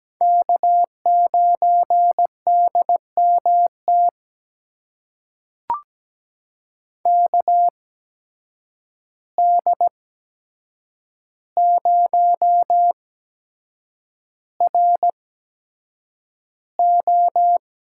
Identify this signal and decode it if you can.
Morse code, heard as a keyed tone, about 17 wpm, spaced as for 7 wpm, 700 Hz. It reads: K9DMT KD0RO
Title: USA Call Signs - 5 Characters - 8x Character Spacing - 17wpm